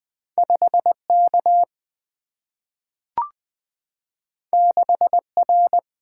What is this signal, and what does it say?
Morse code, heard as a keyed tone, about 20 wpm, 700 Hz. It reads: 5 K 6 R
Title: Single-Letter Single-Number 20wpm